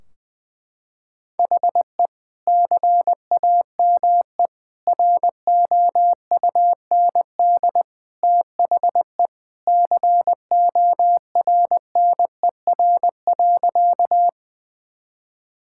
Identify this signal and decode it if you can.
Morse code, heard as a keyed tone, about 20 wpm, 700 Hz. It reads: HE CAME ROUND THE CORNER.
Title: Sentences from Top 1600 Words 20wpm